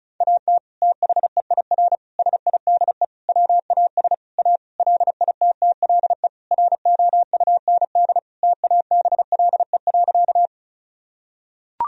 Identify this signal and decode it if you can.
Morse code, heard as a keyed tone, about 34 wpm, 700 Hz. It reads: AT THEIR SIDE WAS A LITTLE ROUND TABLE.
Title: Sentences from Top 1000 Words - Review Part 2 - 35wpm